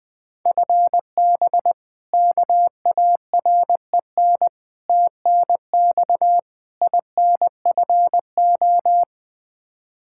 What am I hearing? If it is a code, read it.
Morse code, heard as a keyed tone, about 20 wpm, 700 Hz. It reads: FB KAREN TNX INFO
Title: K1USN SST QSO Practice 20wpm